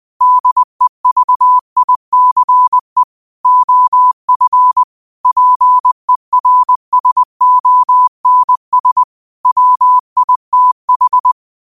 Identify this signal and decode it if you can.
Morse code, heard as a keyed tone, about 20 wpm, 1000 Hz. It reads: DEVICE OF PERSONS WITH
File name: qotdfiles / 20wpm